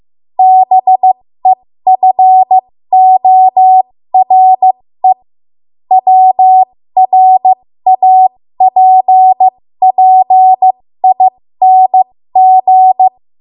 Morse code, heard as a keyed tone, about 15 wpm, spaced as for 13 wpm, 750 Hz. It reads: BEFORE WRAPPING
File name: Archive / 13 WPM / 13WPM